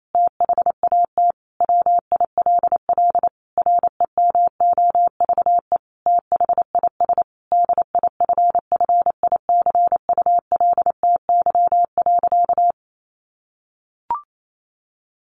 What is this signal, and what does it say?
Morse code, heard as a keyed tone, about 28 wpm, 700 Hz. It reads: THAT WILL REMOVE THIS DIFFICULTY.
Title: Sentences from Top 1500 Words 28wpm